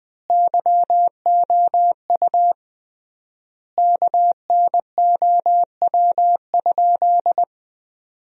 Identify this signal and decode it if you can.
Morse code, heard as a keyed tone, about 20 wpm, 700 Hz. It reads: YOU KNOW?
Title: 3x Word-Spacing - Sentences from Top 100 Words - 20wpm